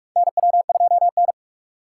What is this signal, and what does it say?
Morse code, heard as a keyed tone, about 46 wpm, 700 Hz. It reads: NW2N
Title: USA Call Signs - 3x Character Spacing - 45wpm